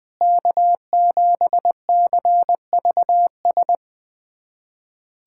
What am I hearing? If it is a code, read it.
Morse code, heard as a keyed tone, about 20 wpm, 700 Hz. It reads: K7CVS
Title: USA Callsign DE Callsign - 5 Characters - 20wpm